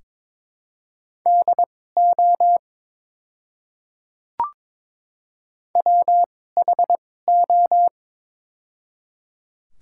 Morse code, heard as a keyed tone, about 22 wpm, spaced as for 16 wpm, 700 Hz. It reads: DO WHO
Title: 2x Character Spacing - Top 100 Words - 22wpm